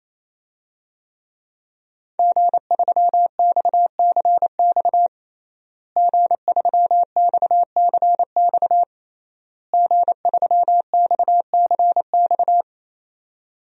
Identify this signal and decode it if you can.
Morse code, heard as a keyed tone, about 28 wpm, 700 Hz. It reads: G3XCX G3XCX G3XCX
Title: DXCC-100 Call Signs - 5 Characters - Repeated 3x - 28wpm